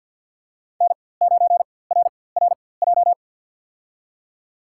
Morse code, heard as a keyed tone, about 50 wpm, spaced as for 23 wpm, 700 Hz. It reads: N9RRJ
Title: USA Call Signs - 5 Characters - 4x Character Spacing - 50wpm